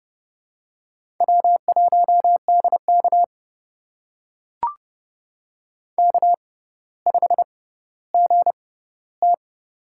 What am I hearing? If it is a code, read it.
Morse code, heard as a keyed tone, about 30 wpm, spaced as for 15 wpm, 700 Hz. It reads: W1DK K5GT
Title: USA Call Signs - 6x Character Spacing - 30wpm